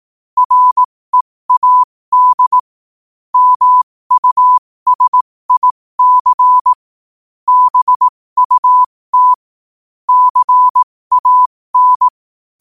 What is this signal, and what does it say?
Morse code, heard as a keyed tone, about 18 wpm, spaced as for 15 wpm, 1000 Hz. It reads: READ MUSIC BUT CAN
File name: qotdfiles / 15wpm